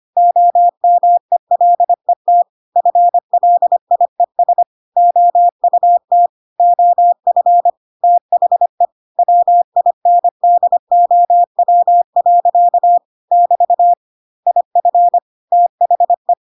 Morse code, heard as a keyed tone, about 25 wpm, 700 Hz. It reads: OMELET FLIES OUT OF THE WINDOW. <BT> IF THE
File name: Pinocchio / Pinocchio ch5 25 25 ik1hge